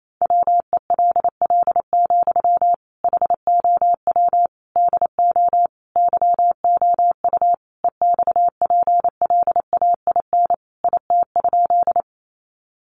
Morse code, heard as a keyed tone, about 28 wpm, 700 Hz. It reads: WELL, HOW DO YOU EXPLAIN IT?
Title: Sentences from Top 1100 Words 28wpm